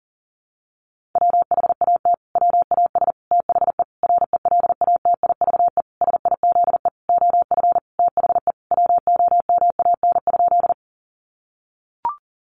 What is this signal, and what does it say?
Morse code, heard as a keyed tone, about 40 wpm, 700 Hz. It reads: WHAT WAS THE RELATIVE SIZE OF THE WOMAN?
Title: Sentences from Top 1400 Words 40wpm